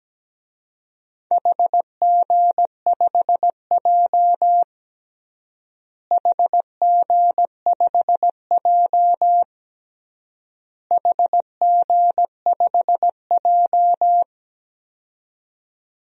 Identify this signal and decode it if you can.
Morse code, heard as a keyed tone, about 17 wpm, 700 Hz. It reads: HG5J HG5J HG5J
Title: DXCC-100 Call Signs - 4 Characters - Repeated 3x - 17wpm